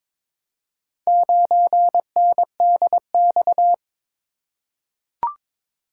Morse code, heard as a keyed tone, about 22 wpm, 700 Hz. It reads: 9NDX
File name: practice / 22wpm / Four Letter-Number Combinations - 22wpm